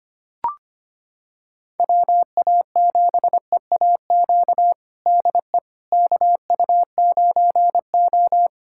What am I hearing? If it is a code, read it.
Morse code, heard as a keyed tone, about 25 wpm, 700 Hz. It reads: WA7EAQ DE KU9O
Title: USA Callsign DE Callsign 25wpm